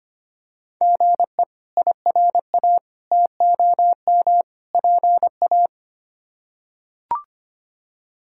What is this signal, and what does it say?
Morse code, heard as a keyed tone, about 25 wpm, 700 Hz. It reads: GE IRA TOM PA
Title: K1USN SST QSO Practice - 25wpm